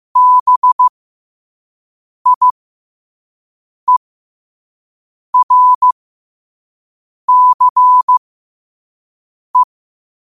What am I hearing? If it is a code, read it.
Morse code, heard as a keyed tone, about 15 wpm, spaced as for 5 wpm, 1000 Hz. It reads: BIERCE
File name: qotdfiles / 5wpm